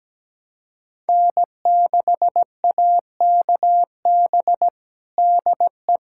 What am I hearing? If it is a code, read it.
Morse code, heard as a keyed tone, about 17 wpm, 700 Hz. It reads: N6AKB DE
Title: USA Callsign DE Callsign 17wpm